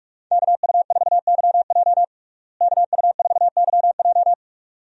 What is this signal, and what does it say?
Morse code, heard as a keyed tone, about 43 wpm, 700 Hz. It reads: KU4YJ KU4YJ
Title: USA Call Signs - 5 Characters - Repeated 2x - 45wpm